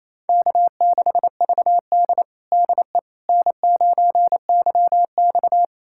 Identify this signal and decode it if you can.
Morse code, heard as a keyed tone, about 28 wpm, 700 Hz. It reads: K6VD DE N9YX
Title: USA Callsign DE Callsign - 4 Characters - 28wpm